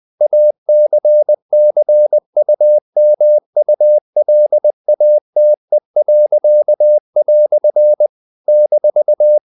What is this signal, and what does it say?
Morse code, heard as a keyed tone, about 20 wpm, 600 Hz. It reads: ACCUMULATE." -
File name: CW / Practice / EnglishQuotations0320